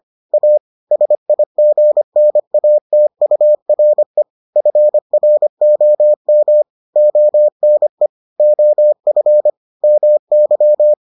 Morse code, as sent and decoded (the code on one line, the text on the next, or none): .- / ... .. --. -. .- - ..- .-. / ..-. .-. --- -- / --- -. / --- ..-. / -- -.--
A SIGNATURE FROM ONE OF MY